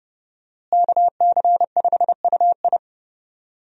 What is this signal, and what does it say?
Morse code, heard as a keyed tone, about 30 wpm, 700 Hz. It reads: KC5UI
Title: USA Call Signs - 5 Characters - Repeated 3x - 30wpm